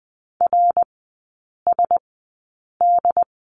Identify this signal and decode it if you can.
Morse code, heard as a keyed tone, about 20 wpm, spaced as for 8 wpm, 690 Hz. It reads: RSD
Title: 159 Lesson 26e - Letter g - Three Characters 20wpm